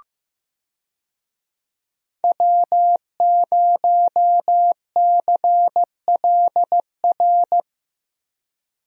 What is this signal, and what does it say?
Morse code, heard as a keyed tone, about 15 wpm, 700 Hz. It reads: W0CLR
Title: USA Call Signs - 5 Characters - Repeated 3x - 15wpm